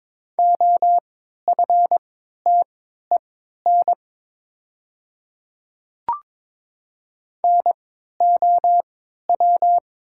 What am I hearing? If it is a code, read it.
Morse code, heard as a keyed tone, about 22 wpm, spaced as for 13 wpm, 700 Hz. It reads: OFTEN NOW
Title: 3x Character Spacing - Top 300 Words - 22wpm